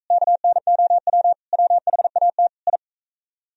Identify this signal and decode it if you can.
Morse code, heard as a keyed tone, about 40 wpm, 700 Hz. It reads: KNOW WHAT I